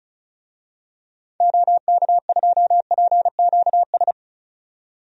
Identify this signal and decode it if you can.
Morse code, heard as a keyed tone, about 36 wpm, 700 Hz. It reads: OK2PQS
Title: DXCC-100 Call Signs - 6 Characters - 35wpm